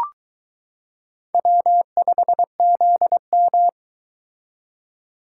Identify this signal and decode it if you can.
Morse code, heard as a keyed tone, about 23 wpm, 700 Hz. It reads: W5ZM